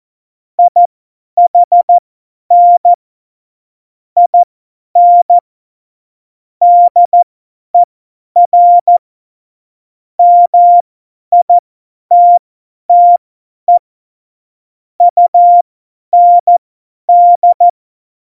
Morse code, heard as a keyed tone, about 14 wpm, spaced as for 10 wpm, 700 Hz. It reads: IHN IN DER MITTE UND
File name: Morsekurs / Witze / Witz 02. Oktober 2018, 50 F70